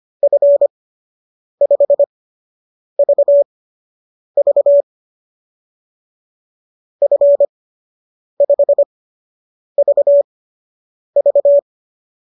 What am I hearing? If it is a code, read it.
Morse code, heard as a keyed tone, about 25 wpm, spaced as for 8 wpm, 590 Hz. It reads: F5VV F5VV